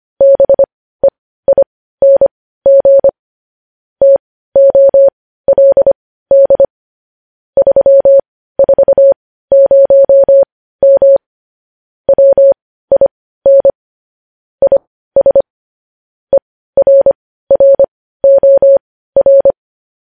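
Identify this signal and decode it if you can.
Morse code, heard as a keyed tone, about 25 wpm, spaced as for 15 wpm, 550 Hz. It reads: BEING TOLD 340M WIN IS ERROR
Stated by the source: Morse code news headlines practice at 15, preceded by vvv.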